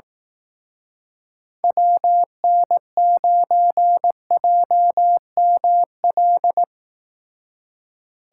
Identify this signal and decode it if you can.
Morse code, heard as a keyed tone, about 18 wpm, 700 Hz. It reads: WN9JML
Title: Speed Racing - USA 2x3 Call Signs 15wpm